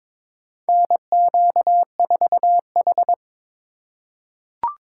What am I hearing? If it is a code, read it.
Morse code, heard as a keyed tone, about 22 wpm, 700 Hz. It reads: NQ4H
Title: USA Call Signs - 4 Characters - 4x Character Spacing - 22wpm